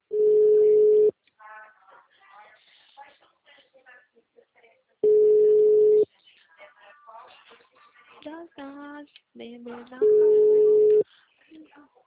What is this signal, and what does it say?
Ringback tone (Europe, 425 Hz), with speech.